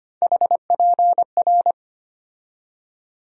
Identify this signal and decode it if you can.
Morse code, heard as a keyed tone, about 25 wpm, 700 Hz. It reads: HPR
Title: USA Call Sign Postfixes - 3 Letters 25wpm